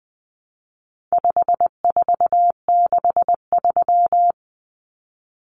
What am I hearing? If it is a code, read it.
Morse code, heard as a keyed tone, about 20 wpm, 700 Hz. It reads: 5463
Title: Numbers - 4-Digit 20wpm